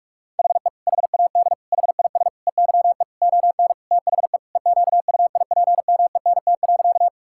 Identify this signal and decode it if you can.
Morse code, heard as a keyed tone, about 46 wpm, 700 Hz. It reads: HE HAD HIS EYE ON THE EQUIPMENT.
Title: Sentences from Top 1000 Words - Review Part 5 - 45wpm